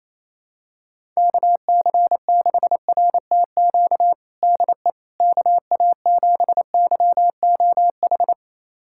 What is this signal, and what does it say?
Morse code, heard as a keyed tone, about 28 wpm, 700 Hz. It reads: KC6RTQ DE KA7YOH
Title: USA Callsign DE Callsign - 6 Characters - 28wpm